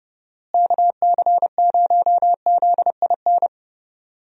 Morse code, heard as a keyed tone, about 30 wpm, 700 Hz. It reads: KC0ZIN
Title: USA Call Signs - 6 Characters - Repeated 3x - 30wpm